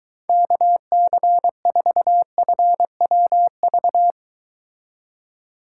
Morse code, heard as a keyed tone, about 23 wpm, 700 Hz. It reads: KC4FWV